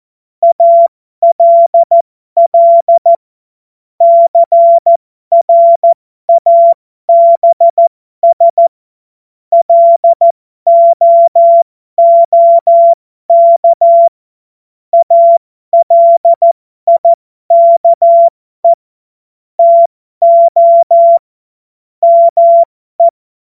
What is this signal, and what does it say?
Morse code, heard as a keyed tone, about 14 wpm, spaced as for 12 wpm, 680 Hz. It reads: ALL CRABS LOOK ALIKE TO ME
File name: Morsekurs / Witze